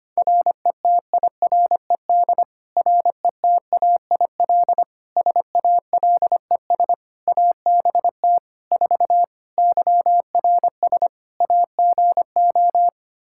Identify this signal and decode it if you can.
Morse code, heard as a keyed tone, about 25 wpm, 700 Hz. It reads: RETIRED RETAIL SALES ABT 4 YRS AGO
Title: Practice QSO - Exchange 3 - More Info - 25wpm